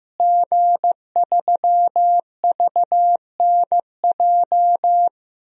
Morse code, heard as a keyed tone, about 15 wpm, 700 Hz. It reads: G3VNJ
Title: DXCC-100 Call Signs - 5 Characters - Repeated 3x - 15wpm